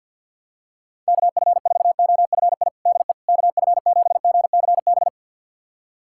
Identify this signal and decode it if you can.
Morse code, heard as a keyed tone, about 52 wpm, 700 Hz. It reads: KF4OFI DE KF7GCB
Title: USA Callsign DE Callsign - 6 Characters - 50wpm